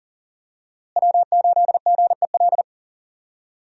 Morse code, heard as a keyed tone, about 40 wpm, 700 Hz. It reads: W8GEL
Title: USA Call Signs - 5 Characters - Repeated 2x - 40wpm